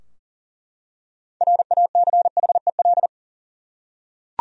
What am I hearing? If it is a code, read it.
Morse code, heard as a keyed tone, about 37 wpm, 700 Hz. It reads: RACHEL
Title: USA Common Names - 40wpm